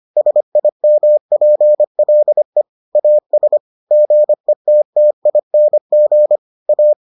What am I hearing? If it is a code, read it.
Morse code, heard as a keyed tone, about 25 wpm, 600 Hz. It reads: SIMPLE AS GETTING A